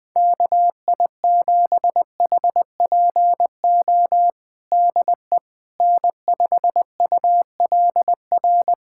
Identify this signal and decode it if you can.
Morse code, heard as a keyed tone, about 21 wpm, 700 Hz. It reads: KI7HPO DE N5ULR
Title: USA Callsign DE Callsign 20wpm